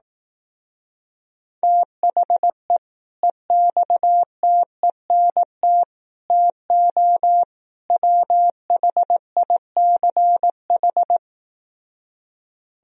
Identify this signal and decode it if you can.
Morse code, heard as a keyed tone, about 18 wpm, 700 Hz. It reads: THE EXTENT TO WHICH
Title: Speed-Racing - Sets of 4 Words 15wpm